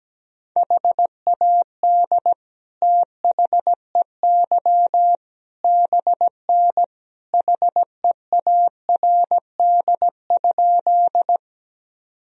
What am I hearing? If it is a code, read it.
Morse code, heard as a keyed tone, about 17 wpm, 700 Hz. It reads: HAD THEY BN HEARD?
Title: Sentences from Top 400 Words and Abbreviations 17wpm